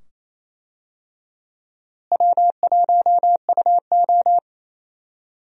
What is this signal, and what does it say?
Morse code, heard as a keyed tone, about 27 wpm, 700 Hz. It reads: W1UO
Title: USA Call Signs - 4 Characters - Repeated 2x - 28wpm